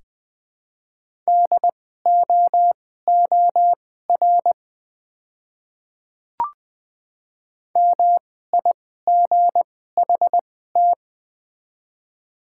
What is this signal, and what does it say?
Morse code, heard as a keyed tone, about 20 wpm, spaced as for 14 wpm, 700 Hz. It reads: DOOR MIGHT
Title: 2x Character Spacing - Top 300 Words - 20wpm